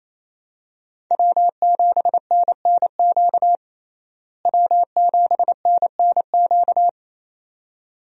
Morse code, heard as a keyed tone, about 28 wpm, 700 Hz. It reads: W7NNQ W7NNQ
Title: USA Call Signs - 5 Characters - Repeated 2x - 28wpm